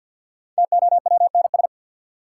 Morse code, heard as a keyed tone, about 49 wpm, 700 Hz. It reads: TOWNS
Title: Sentences from Top 1000 Words - Review Part 6 - New Word Drill - 50wpm